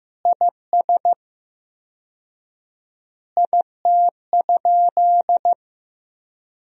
Morse code, heard as a keyed tone, about 15 wpm, 700 Hz. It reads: IS IT?
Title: 4x Word-Spacing - Sentences from Top 200 Words - 15wpm